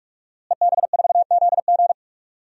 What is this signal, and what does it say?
Morse code, heard as a keyed tone, about 42 wpm, 700 Hz. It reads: EB4ZG
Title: DXCC-100 Call Signs - 5 Characters - Repeated 3x - 45wpm